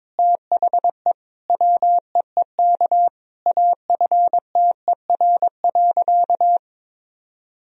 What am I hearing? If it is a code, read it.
Morse code, heard as a keyed tone, about 22 wpm, 700 Hz. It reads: THE WEEK AFTER.
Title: Sentences from Top 500 Words and Abbreviations 22wpm